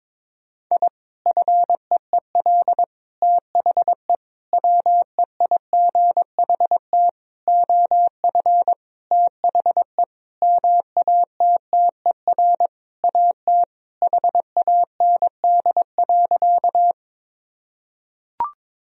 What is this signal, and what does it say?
Morse code, heard as a keyed tone, about 22 wpm, 700 Hz. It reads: I FEEL THE WEIGHT OF THE MATTER AT HAND.
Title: Sentences from Top 1000 Words - Review Part 3 - 22wpm